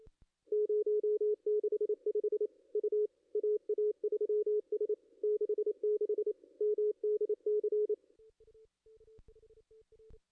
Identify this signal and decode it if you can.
Morse code, heard as a keyed tone, about 28 wpm, 420 Hz. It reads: 065 U AA3S 66 MDC
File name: contests / ss-cw-k5tr-op-n5rz -2300